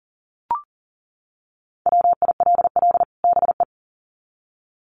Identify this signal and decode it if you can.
Morse code, heard as a keyed tone, about 40 wpm, 700 Hz. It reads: WILL BE
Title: Sets of 2 Words 40wpm